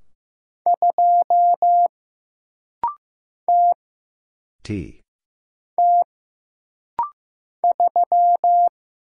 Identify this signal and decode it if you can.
Morse code, heard as a keyed tone, about 15 wpm, 700 Hz. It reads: 2 T T 3
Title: Single Letter-Number Alt - Rapid-Fire 15wpm